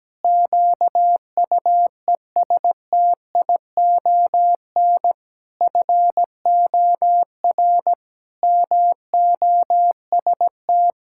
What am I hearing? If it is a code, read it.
Morse code, heard as a keyed tone, about 17 wpm, 700 Hz. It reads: QUESTION FOR MOST
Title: Pangram Challenge v2 - 17wpm